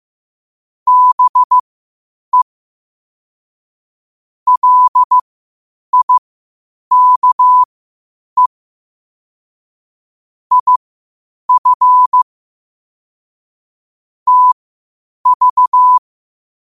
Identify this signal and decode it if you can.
Morse code, heard as a keyed tone, about 15 wpm, spaced as for 8 wpm, 1000 Hz. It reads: BE LIKE IF TV
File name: qotdfiles / 8wpm